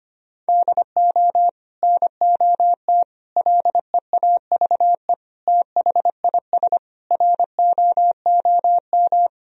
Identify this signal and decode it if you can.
Morse code, heard as a keyed tone, about 25 wpm, 700 Hz. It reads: DO NOT LEAVE THIS ROOM
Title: Sentences from Top 1000 Words - Review Part 1 - 25wpm